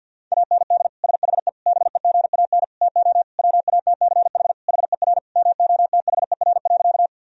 Morse code, heard as a keyed tone, about 50 wpm, 700 Hz. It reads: AND SHE BEGAN TO WATCH HER MOTHER.
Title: Sentences from Top 1000 Words - Review Part 1 - 50wpm